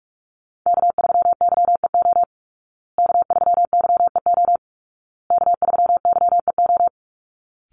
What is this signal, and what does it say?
Morse code, heard as a keyed tone, about 44 wpm, 700 Hz. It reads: K3YEO K3YEO K3YEO
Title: Call Signs Repeated 2x - 45wpm